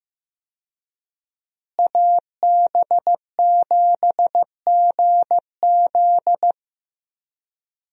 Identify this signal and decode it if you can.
Morse code, heard as a keyed tone, about 15 wpm, 700 Hz. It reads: AB7GZ
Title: USA Call Signs - 5 Characters - Repeated 2x - 15wpm